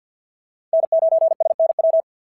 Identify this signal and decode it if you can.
Morse code, heard as a keyed tone, about 53 wpm, 690 Hz. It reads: N9SNW
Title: SOTA QSO Practice 50wpm